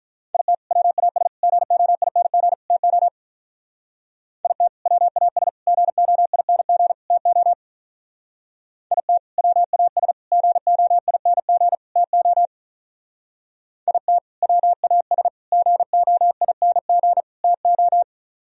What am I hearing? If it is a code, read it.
Morse code, heard as a keyed tone, about 44 wpm, 700 Hz. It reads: IT WAS GOING TO IT WAS GOING TO IT WAS GOING TO IT WAS GOING TO
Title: Speed-Racing - Sets of 4 Words 35wpm